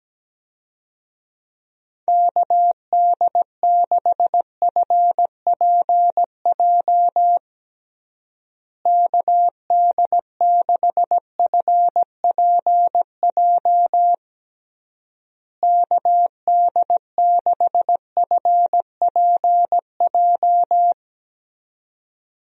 Morse code, heard as a keyed tone, about 17 wpm, 700 Hz. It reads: KD6FPJ KD6FPJ KD6FPJ
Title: USA Call Signs - 6 Characters - Repeated 3x - 17wpm